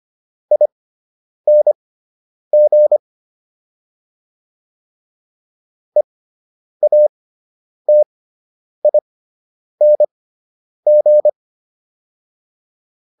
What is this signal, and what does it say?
Morse code, heard as a keyed tone, about 25 wpm, spaced as for 9 wpm, 600 Hz. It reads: ING EATING